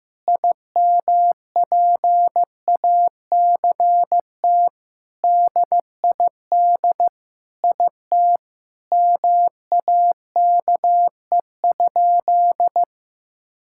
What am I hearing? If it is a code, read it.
Morse code, heard as a keyed tone, about 15 wpm, 700 Hz. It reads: IMPACT DID IT MAKE?
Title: Sentences from Top 1000 Words - Review Part 6 - 15wpm